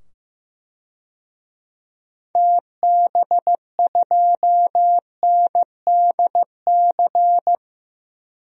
Morse code, heard as a keyed tone, about 15 wpm, 700 Hz. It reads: TB2NDC
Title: DXCC-100 Call Signs - 6 Characters - 15wpm